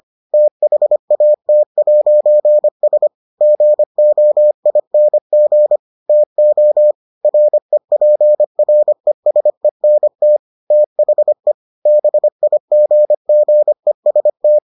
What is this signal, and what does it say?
Morse code, heard as a keyed tone, about 25 wpm, 600 Hz. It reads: THAT'S GOING TO REPRESENT THE BIGGEST